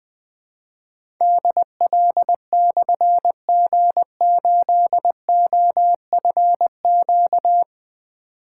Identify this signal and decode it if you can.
Morse code, heard as a keyed tone, about 20 wpm, 700 Hz. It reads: DL/G8OFQ
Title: Difficult Call Signs 20wpm